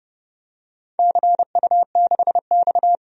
Morse code, heard as a keyed tone, about 30 wpm, 700 Hz. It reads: CU6X
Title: DXCC-100 Call Signs - 4 Characters - 30wpm